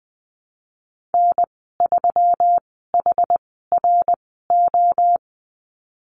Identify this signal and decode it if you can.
Morse code, heard as a keyed tone, about 20 wpm, spaced as for 15 wpm, 700 Hz. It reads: N3HRO
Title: 173 Lesson 28g - Number 9 - Call Signs 20wpm